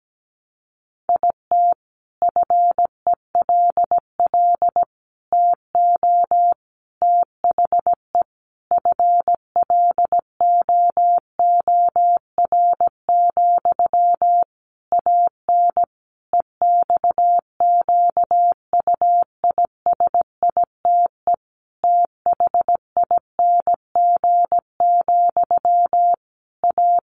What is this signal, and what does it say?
Morse code, heard as a keyed tone, about 17 wpm, 700 Hz. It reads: IT FELL TO THE FLOOR, AN EXQUISITE THING, A
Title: A Sound Of Thunder 17wpm